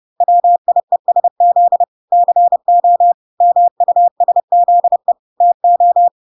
Morse code, heard as a keyed tone, about 30 wpm, 700 Hz. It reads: WIESZ CO MUSZE TO